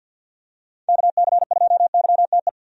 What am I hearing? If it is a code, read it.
Morse code, heard as a keyed tone, about 46 wpm, 700 Hz. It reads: KC2YTE
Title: USA Call Signs - 6 Characters - 50wpm